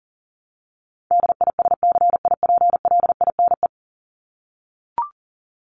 Morse code, heard as a keyed tone, about 40 wpm, 700 Hz. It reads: DISCIPLINE
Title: Top 751-1000 Words 40wpm